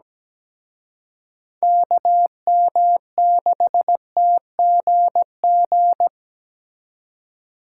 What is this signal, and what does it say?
Morse code, heard as a keyed tone, about 20 wpm, 700 Hz. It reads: KM6TGG
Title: Speed Racing - USA 2x3 Call Signs 17wpm